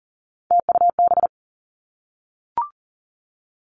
Morse code, heard as a keyed tone, about 41 wpm, 700 Hz. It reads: TUB
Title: Three Challenging Letter-Number Combinations 40wpm